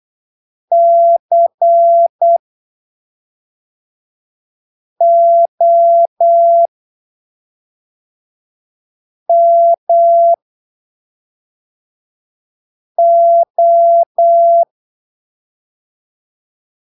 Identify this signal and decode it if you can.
Morse code, heard as a keyed tone, about 8 wpm, spaced as for 3 wpm, 680 Hz. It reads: COMO
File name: arquivos / texto / texto 18 08ppm